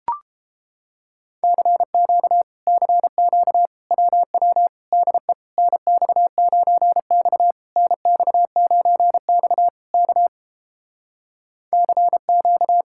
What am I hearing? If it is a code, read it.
Morse code, heard as a keyed tone, about 30 wpm, 700 Hz. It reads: CQ CQ WW DE NX9X NX9X K CQ